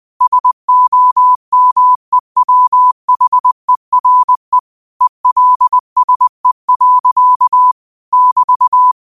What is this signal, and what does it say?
Morse code, heard as a keyed tone, about 20 wpm, 1000 Hz. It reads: SOMEWHERE ELSE. <BT>
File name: qotdfiles / 20wpm